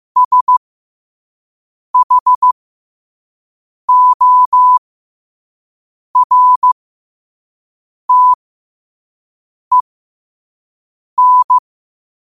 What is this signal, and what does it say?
Morse code, heard as a keyed tone, about 15 wpm, spaced as for 5 wpm, 1000 Hz. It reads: SHORTEN
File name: qotdfiles / 5wpm